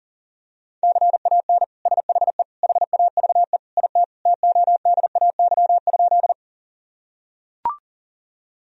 Morse code, heard as a keyed tone, about 40 wpm, 700 Hz. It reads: CAN SHE HAVE IT TODAY?
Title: Sentences from Top 1000 Words - Review Part 4 - 40wpm